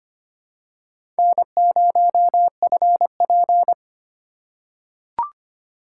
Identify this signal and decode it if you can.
Morse code, heard as a keyed tone, about 25 wpm, 700 Hz. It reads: N0FP
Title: USA Call Signs - 4 Characters - 4x Character Spacing - 25wpm